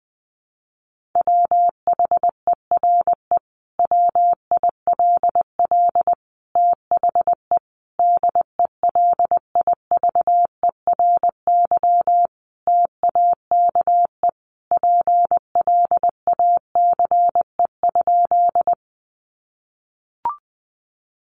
Morse code, heard as a keyed tone, about 20 wpm, 700 Hz. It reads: WHERE WILL THE DELIVERY TAKE PLACE?
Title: Sentences from Top 1500 Words 20wpm